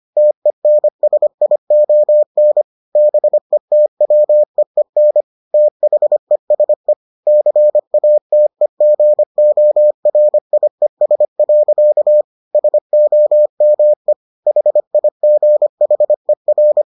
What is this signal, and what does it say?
Morse code, heard as a keyed tone, about 25 wpm, 600 Hz. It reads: TENSION BETWEEN THESE CATEGORIES. SOME HIGHER